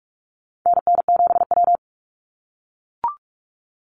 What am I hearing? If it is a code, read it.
Morse code, heard as a keyed tone, about 44 wpm, 700 Hz. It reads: NN7W
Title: Call Signs - 45wpm